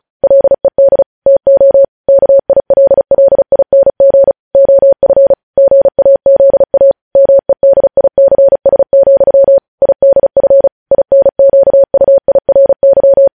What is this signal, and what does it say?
Morse code, heard as a keyed tone, about 36 wpm, 550 Hz. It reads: LED TO KILLING OF GAZA MEDICS, IDF INQUIRY
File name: podcast / News headlines-1x-35x35